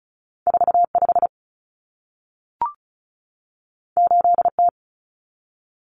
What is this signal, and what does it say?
Morse code, heard as a keyed tone, about 36 wpm, 700 Hz. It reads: 45 8T
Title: Common Cut Numbers - 2-Digit 35wpm